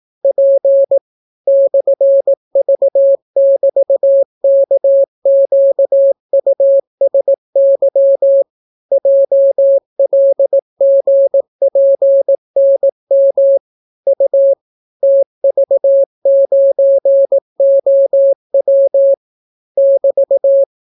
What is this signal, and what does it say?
Morse code, heard as a keyed tone, about 18 wpm, 550 Hz. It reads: P /V<BT>KQUSY JLGPNM U TV9OW <BT>